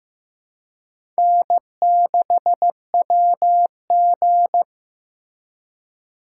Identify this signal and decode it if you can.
Morse code, heard as a keyed tone, about 16 wpm, 700 Hz. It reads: N6WG
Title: USA Call Signs - 4 Characters - 2x Character Spacing - 15wpm